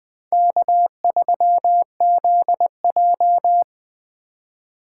Morse code, heard as a keyed tone, about 21 wpm, 700 Hz. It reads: K3ZJ
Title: Speed Racing - USA 1x2 Call Signs 17wpm